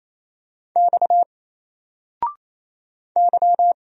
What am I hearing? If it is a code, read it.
Morse code, heard as a keyed tone, about 27 wpm, 700 Hz. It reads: X Y
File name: Single-Letter-Number-28wpm